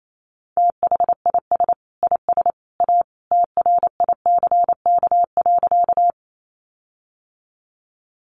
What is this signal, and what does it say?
Morse code, heard as a keyed tone, about 28 wpm, 700 Hz. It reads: THIS IS A TRICK.
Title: Sentences from Top 1500 Words 28wpm